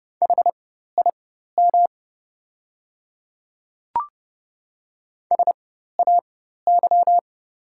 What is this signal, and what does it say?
Morse code, heard as a keyed tone, about 30 wpm, spaced as for 14 wpm, 700 Hz. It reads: HIM SAY
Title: 4x Character Spacing - Top 200 Words - 30wpm